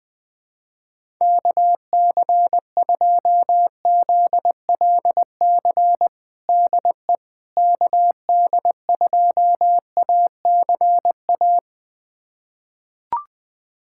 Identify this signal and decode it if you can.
Morse code, heard as a keyed tone, about 21 wpm, 700 Hz. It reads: KC2ZLC DE KD2ACA
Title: USA Callsign DE Callsign 20wpm